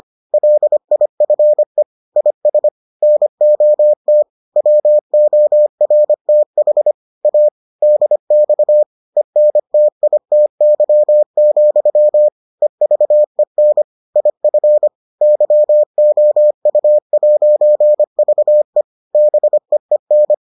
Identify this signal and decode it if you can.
Morse code, heard as a keyed tone, about 25 wpm, 600 Hz. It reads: LIFE IS NOT WORTH A DX ENTITY, EVEN IF YOU'VE BEEN